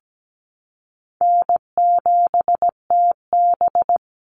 Morse code, heard as a keyed tone, about 17 wpm, 700 Hz. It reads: N7TB
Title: Easy Call Signs Repeated 3x - 17wpm